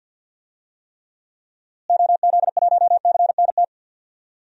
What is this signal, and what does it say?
Morse code, heard as a keyed tone, about 47 wpm, 700 Hz. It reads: OZ1CNT
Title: DXCC-100 Call Signs - 6 Characters - Repeated 2x - 50wpm